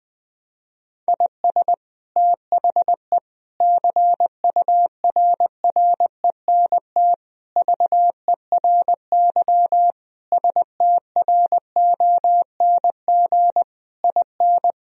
Morse code, heard as a keyed tone, about 20 wpm, 700 Hz. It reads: IS THE CURRENT VERY STRONG IN
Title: Sentences from Top 1000 Words - Review Part 4 - 20wpm